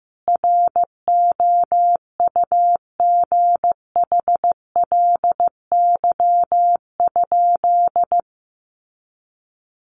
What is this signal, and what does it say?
Morse code, heard as a keyed tone, about 15 wpm, 700 Hz. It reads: ROUGHLY?
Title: Sentences from Top 1500 Words 15wpm